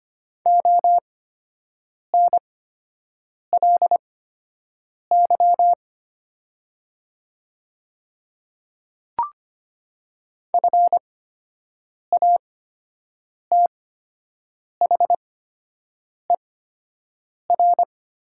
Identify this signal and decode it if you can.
Morse code, heard as a keyed tone, about 25 wpm, spaced as for 7 wpm, 700 Hz. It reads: ONLY FATHER
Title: 8x Character Spacing - Top 200 Words - 25wpm